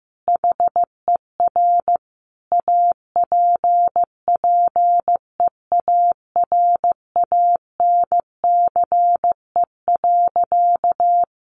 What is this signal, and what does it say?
Morse code, heard as a keyed tone, about 15 wpm, 700 Hz. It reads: HER APPEARANCE.
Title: Sentences from Top 1100 Words 15wpm